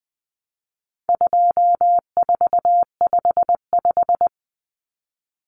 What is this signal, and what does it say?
Morse code, heard as a keyed tone, about 20 wpm, 700 Hz. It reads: 2455
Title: Common Cut Numbers - 4-Digit 20wpm